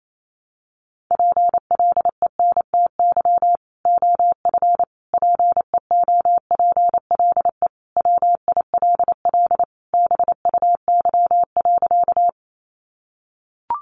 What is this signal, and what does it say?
Morse code, heard as a keyed tone, about 28 wpm, 700 Hz. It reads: PLENTY OF PEOPLE WILL BUY.
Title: Sentences from Top 1300 Words 28wpm